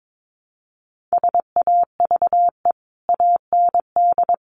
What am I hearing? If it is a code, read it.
Morse code, heard as a keyed tone, about 22 wpm, 700 Hz. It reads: SAVE AND
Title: Sentences from Top 1100 Words 22wpm